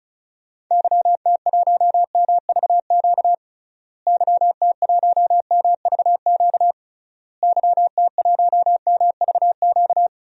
Morse code, heard as a keyed tone, about 36 wpm, 700 Hz. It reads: YT1MVQ YT1MVQ YT1MVQ
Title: DXCC-100 Call Signs - 6 Characters - Repeated 3x - 35wpm